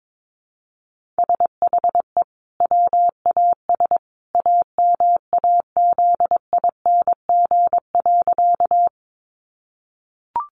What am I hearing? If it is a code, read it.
Morse code, heard as a keyed tone, about 22 wpm, 700 Hz. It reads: SHE WAS AMAZING.
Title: Sentences from Top 1500 Words 22wpm